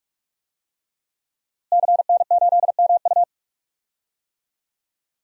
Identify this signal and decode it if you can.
Morse code, heard as a keyed tone, about 43 wpm, 700 Hz. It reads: CN8MU
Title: DXCC-100 Call Signs - 5 Characters - Repeated 3x - 45wpm